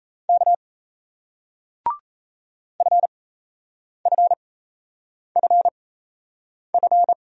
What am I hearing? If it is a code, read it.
Morse code, heard as a keyed tone, about 39 wpm, spaced as for 8 wpm, 700 Hz. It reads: K FFFF